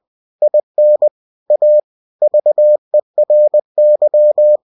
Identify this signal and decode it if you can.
Morse code, heard as a keyed tone, about 20 wpm, 600 Hz. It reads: IN A VERY
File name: cw-books / winnie-the-pooh / winnie-20wpm-page-0020